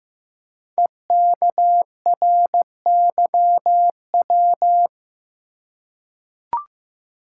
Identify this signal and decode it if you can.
Morse code, heard as a keyed tone, about 15 wpm, 700 Hz. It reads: EKRYW
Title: Code Groups - 5 Letters - 15wpm